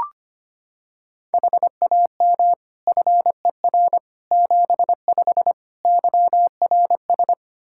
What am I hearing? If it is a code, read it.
Morse code, heard as a keyed tone, about 25 wpm, 700 Hz. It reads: HAM FER 75 YRS
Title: K1USN SST QSO Practice 25wpm